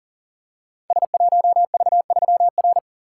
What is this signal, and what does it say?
Morse code, heard as a keyed tone, about 39 wpm, 700 Hz. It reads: S1V3R
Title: Code Groups - 5 Letter:Numbers - 40wpm